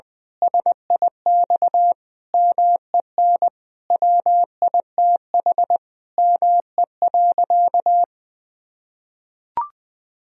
Morse code, heard as a keyed tone, about 21 wpm, 700 Hz. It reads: SIX MEN WITH ME.
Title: Sentences from Top 1000 Words - Review Part 2 - 20wpm